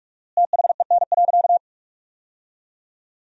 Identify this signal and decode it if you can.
Morse code, heard as a keyed tone, about 42 wpm, 700 Hz. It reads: THEN.
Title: 4x Word-Spacing - Sentences from Top 200 Words - 45wpm